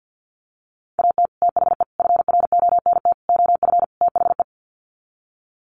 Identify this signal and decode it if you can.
Morse code, heard as a keyed tone, about 50 wpm, 700 Hz. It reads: AT THE FRONT OF THE
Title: Sets of 5 Words 50wpm